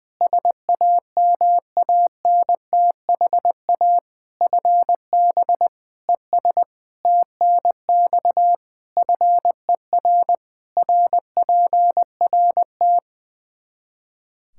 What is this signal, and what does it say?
Morse code, heard as a keyed tone, about 20 wpm, 700 Hz. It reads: SAMANTHA FB ES TNX FER RPRT
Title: Practice QSO - Exchange 2 - Info - 20wpm